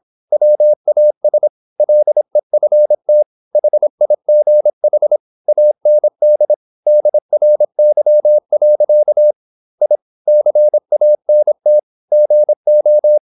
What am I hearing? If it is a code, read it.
Morse code, heard as a keyed tone, about 26 wpm, 600 Hz. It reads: WAS LEFT HIGH AND DRY. I CANT GO
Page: Never 26wpm | CW med Gnister